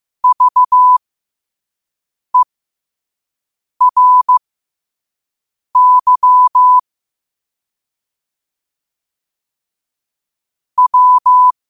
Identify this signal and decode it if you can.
Morse code, heard as a keyed tone, about 15 wpm, spaced as for 5 wpm, 1000 Hz. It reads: VERY W
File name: qotdfiles / 5wpm